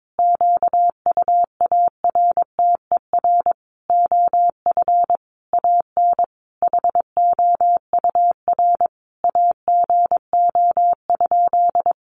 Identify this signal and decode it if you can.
Morse code, heard as a keyed tone, about 22 wpm, 700 Hz. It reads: QUARTER OF AN HOUR AGO?
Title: Sentences from Top 1300 Words 22wpm